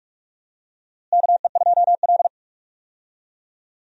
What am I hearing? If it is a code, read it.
Morse code, heard as a keyed tone, about 44 wpm, 700 Hz. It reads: KE2L
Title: USA Call Signs - 4 Characters - Repeated 3x - 45wpm